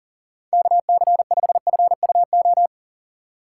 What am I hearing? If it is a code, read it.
Morse code, heard as a keyed tone, about 37 wpm, 700 Hz. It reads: KC5FUO
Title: USA Call Signs - 6 Characters - 4x Character Spacing - 40wpm